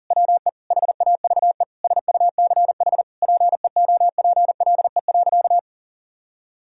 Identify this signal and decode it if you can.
Morse code, heard as a keyed tone, about 40 wpm, 700 Hz. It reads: WE HAVE SUCH PEOPLE.
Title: Sentences from Top 1000 Words - Review Part 3 - 40wpm